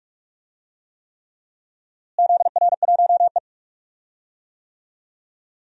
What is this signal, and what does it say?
Morse code, heard as a keyed tone, about 42 wpm, 700 Hz. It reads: ZR1E
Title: DXCC-100 Call Signs - 4 Characters - Repeated 3x - 45wpm